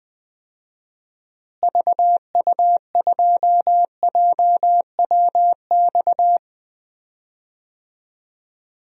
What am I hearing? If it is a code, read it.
Morse code, heard as a keyed tone, about 20 wpm, 700 Hz. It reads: VU2JWX
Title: DXCC-100 Call Signs - 6 Characters - Repeated 2x - 20wpm